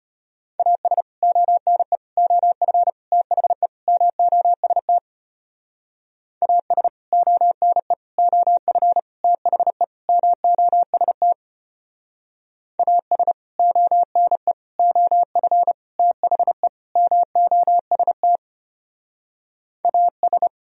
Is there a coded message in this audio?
Yes: Morse code from 0.6 to 20.5 s, heard as a keyed tone, about 33 wpm, 700 Hz. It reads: AS ONE OF THE MOST AS ONE OF THE MOST AS ONE OF THE MOST AS